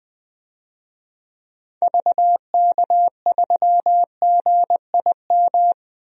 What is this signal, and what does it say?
Morse code, heard as a keyed tone, about 21 wpm, 700 Hz. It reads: VK3GIM
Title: DXCC-100 Call Signs - 6 Characters - 20wpm